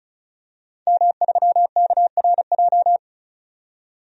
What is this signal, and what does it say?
Morse code, heard as a keyed tone, about 36 wpm, 700 Hz. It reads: M3KRJ
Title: DXCC-100 Call Signs - 5 Characters - 35wpm